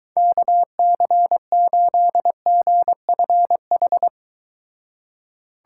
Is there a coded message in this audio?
Yes: Morse code from 0.2 to 4.1 s, heard as a keyed tone, about 23 wpm, 700 Hz. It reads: KC8GFH